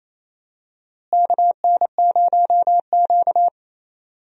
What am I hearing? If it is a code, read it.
Morse code, heard as a keyed tone, about 28 wpm, 700 Hz. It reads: KN0Q
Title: USA Call Signs - 4 Characters - Repeated 3x - 28wpm